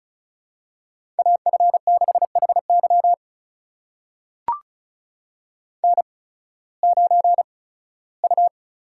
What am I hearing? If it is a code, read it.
Morse code, heard as a keyed tone, about 36 wpm, spaced as for 19 wpm, 700 Hz. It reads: AF6HY N9U
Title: USA Call Signs - 5 Characters - 8x Character Spacing - 35wpm